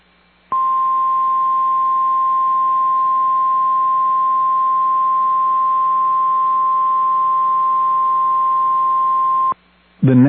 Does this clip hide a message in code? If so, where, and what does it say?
Yes: Emergency Alert System (SAME) tones (the 1050 Hz attention tone) from 0.5 to 9.6 s.